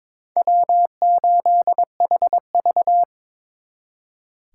Morse code, heard as a keyed tone, about 26 wpm, 700 Hz. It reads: W8HV
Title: Speed Racing - USA Call Signs 22wpm